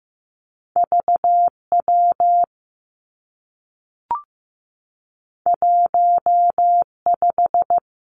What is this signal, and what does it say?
Morse code, heard as a keyed tone, about 15 wpm, 700 Hz. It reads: V W 1 5
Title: Two Challenging Letter-Number Combinations 15wpm